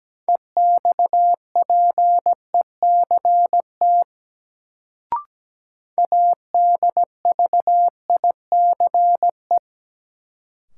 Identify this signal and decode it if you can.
Morse code, heard as a keyed tone, about 17 wpm, 700 Hz. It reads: EXPECT ADVICE
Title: Sentences from Top 1100 Words - New Word Drill Alt - 17wpm